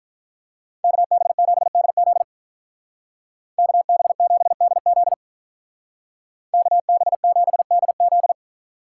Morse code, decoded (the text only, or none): KB7DZ KB7DZ KB7DZ